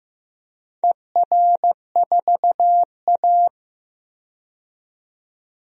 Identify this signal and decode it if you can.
Morse code, heard as a keyed tone, about 15 wpm, 700 Hz. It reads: ER4A
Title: DXCC-100 Call Signs - 4 Characters - 15wpm